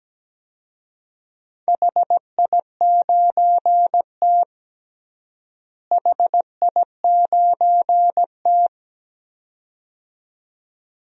Morse code, heard as a keyed tone, about 17 wpm, 700 Hz. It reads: HI9T HI9T
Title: DXCC-100 Call Signs - 4 Characters - Repeated 2x - 17wpm